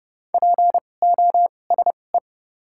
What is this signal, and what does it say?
Morse code, heard as a keyed tone, about 30 wpm, spaced as for 22 wpm, 700 Hz. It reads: POSE
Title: 2x Character Spacing - Top 500 Words - 30wpm